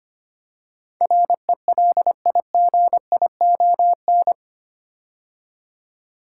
Morse code, heard as a keyed tone, about 25 wpm, 700 Hz. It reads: RELIGION
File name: 8 Letter Words - First 3 Letters Given - 25wpm